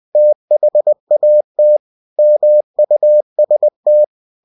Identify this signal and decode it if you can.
Morse code, heard as a keyed tone, about 20 wpm, 600 Hz. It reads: THAT MUST
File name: cw-books / winnie-the-pooh / winnie-20wpm-page-0036